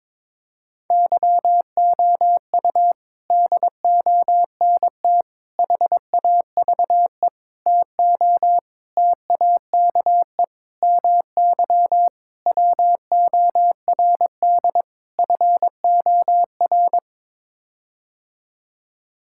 Morse code, heard as a keyed tone, about 22 wpm, 700 Hz. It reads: YOU DONT HAVE TO TAKE MY WORD FOR
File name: practice / 22wpm / Sets of 8 Words - 22wpm